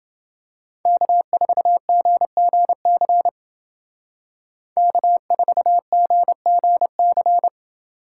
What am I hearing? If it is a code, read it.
Morse code, heard as a keyed tone, about 29 wpm, 700 Hz. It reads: K4GGC K4GGC